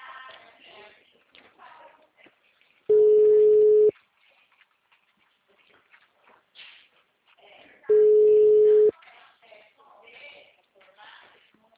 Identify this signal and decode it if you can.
Ringback tone (Europe, 425 Hz), with speech.
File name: out- -2008- - - .0